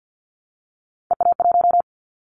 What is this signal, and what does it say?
Morse code, heard as a keyed tone, about 50 wpm, 700 Hz. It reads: EA1
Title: Callsign Prefixes for Top 25 DXCC - 50wpm